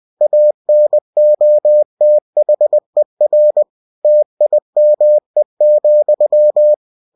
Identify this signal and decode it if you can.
Morse code, heard as a keyed tone, about 20 wpm, 600 Hz. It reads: ANOTHER TIME,
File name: cw-books / winnie-the-pooh / winnie-20wpm-page-0084